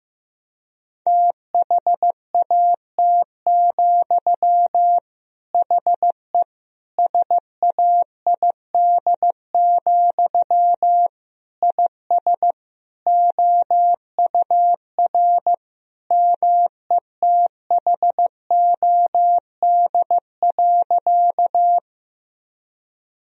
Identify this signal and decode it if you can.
Morse code, heard as a keyed tone, about 15 wpm, 700 Hz. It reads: THAT, HE SAID, IS OUR METHOD.
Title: Sentences from Top 1000 Words - Review Part 3 - 15wpm